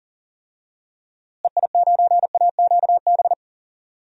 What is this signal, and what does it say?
Morse code, heard as a keyed tone, about 40 wpm, 700 Hz. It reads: EI9AQB
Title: DXCC-100 Call Signs - 6 Characters - Repeated 2x - 40wpm